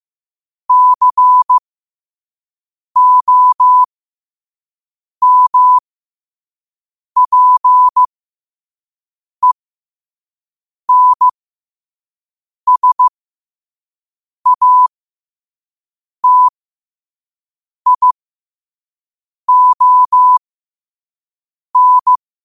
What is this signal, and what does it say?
Morse code, heard as a keyed tone, about 15 wpm, spaced as for 5 wpm, 1000 Hz. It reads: COMPENSATION